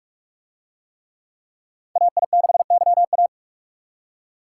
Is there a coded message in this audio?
Yes: Morse code from 2.0 to 3.3 s, heard as a keyed tone, about 41 wpm, 700 Hz. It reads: AI6YA